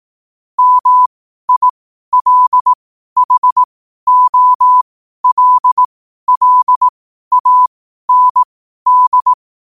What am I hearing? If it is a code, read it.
Morse code, heard as a keyed tone, about 18 wpm, spaced as for 12 wpm, 1000 Hz. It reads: MILHOLLAND
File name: qotdfiles / 12wpm